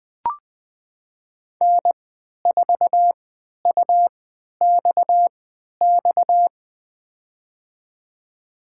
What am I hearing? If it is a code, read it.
Morse code, heard as a keyed tone, about 20 wpm, spaced as for 11 wpm, 700 Hz. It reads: N4UXX
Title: USA Call Signs - 3x Character Spacing - 20wpm